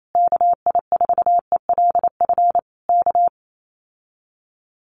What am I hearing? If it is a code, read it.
Morse code, heard as a keyed tone, about 28 wpm, 700 Hz. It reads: KI4ELF K
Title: Common QSO Phrases Encore 28wpm